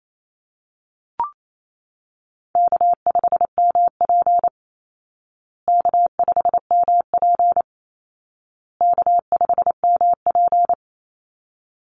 Morse code, heard as a keyed tone, about 28 wpm, 700 Hz. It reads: K5MP K5MP K5MP
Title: Easy Call Signs Repeated 3x - 28wpm